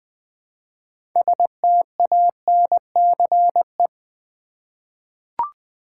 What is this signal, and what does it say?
Morse code, heard as a keyed tone, about 20 wpm, 700 Hz. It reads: STANCE
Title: 6 Letter Words - First 3 Letters Given - 20wpm